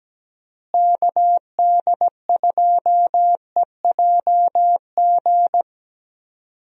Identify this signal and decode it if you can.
Morse code, heard as a keyed tone, about 17 wpm, 700 Hz. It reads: KD2EJG
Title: USA Call Signs - 6 Characters - Repeated 3x - 17wpm